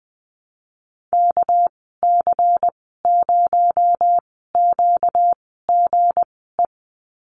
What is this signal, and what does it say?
Morse code, heard as a keyed tone, about 20 wpm, spaced as for 15 wpm, 700 Hz. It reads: KC0QGE
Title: 238 Lesson 38g - Letter q - Call Signs 20wpm